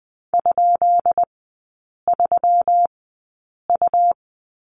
Morse code, heard as a keyed tone, about 20 wpm, spaced as for 8 wpm, 700 Hz. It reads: ?3U
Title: 133 Lesson 22e - Question Mark - Three Characters 20wpm